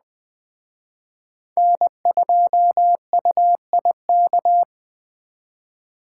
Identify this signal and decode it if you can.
Morse code, heard as a keyed tone, about 21 wpm, 700 Hz. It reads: N2UIK
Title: Speed Racing - USA 1x3 Call Signs 17wpm